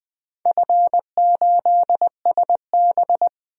Morse code, heard as a keyed tone, about 20 wpm, 700 Hz. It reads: F8SB
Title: DXCC-100 Call Signs - 4 Characters - 20wpm